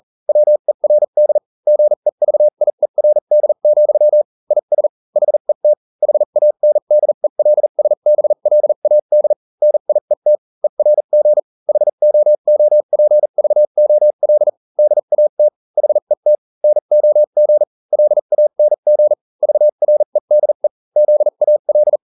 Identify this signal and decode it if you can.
Morse code, heard as a keyed tone, about 40 wpm, spaced as for 35 wpm, 600 Hz. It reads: WERD GEVIERD, IS HET HANDELSBLAD NIET ERG HOOPVOL DAT HET NOG LANG VREDE ZAL